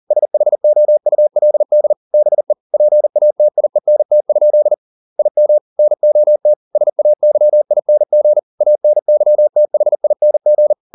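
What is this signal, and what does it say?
Morse code, heard as a keyed tone, about 40 wpm, 600 Hz. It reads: SHOULD BE PATIENT? IM NOT SAYING ANYTHING
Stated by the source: Morse code at 40 wpm, practice text, CW med Gnister.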